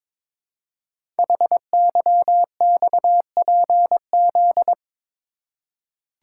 Morse code, heard as a keyed tone, about 22 wpm, 700 Hz. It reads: HYXPZ
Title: Code Groups - 5 Letters - 22wpm